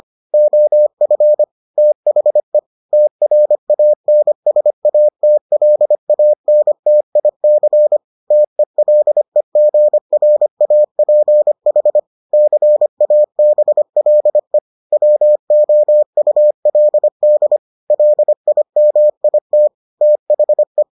Morse code, heard as a keyed tone, about 25 wpm, 600 Hz. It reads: OF THE TRANSATLANTIC TELEGRAPH CABLE WOULD LIMIT THE